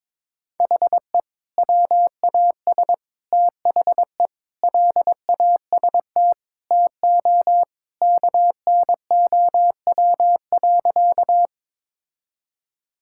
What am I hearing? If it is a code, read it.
Morse code, heard as a keyed tone, about 22 wpm, 700 Hz. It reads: HE WAS THE LAST TO KNOW.
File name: practice / 22wpm / Sentences with 2-Word Phrases - 22wpm